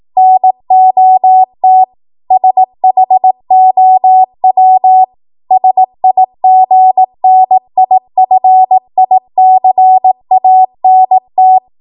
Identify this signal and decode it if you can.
Morse code, heard as a keyed tone, about 18 wpm, 750 Hz. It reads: NOT SHOW SIGNIFICANT
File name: Morse / 18WPM